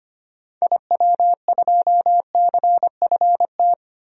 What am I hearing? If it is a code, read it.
Morse code, heard as a keyed tone, about 25 wpm, 700 Hz. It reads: IW2CFT
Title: DXCC-100 Call Signs - 6 Characters - 25wpm